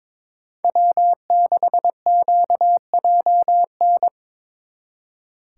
Morse code, heard as a keyed tone, about 25 wpm, 700 Hz. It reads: W6QJN
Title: Speed Racing - USA 1x3 Call Signs 22wpm